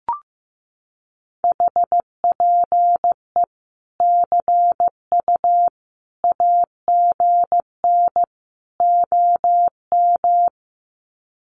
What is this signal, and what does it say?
Morse code, heard as a keyed tone, about 15 wpm, 700 Hz. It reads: HPE CU AGN OM
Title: Common QSO Phrases 15wpm